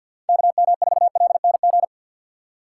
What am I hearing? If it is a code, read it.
Morse code, heard as a keyed tone, about 52 wpm, 700 Hz. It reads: KM4LNG
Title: USA Call Signs - 6 Characters - 4x Character Spacing - 50wpm